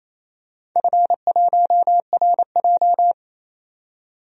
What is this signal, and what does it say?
Morse code, heard as a keyed tone, about 28 wpm, 690 Hz. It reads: F1RJ
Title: DXCC-100 Call Signs - 4 Characters - 28wpm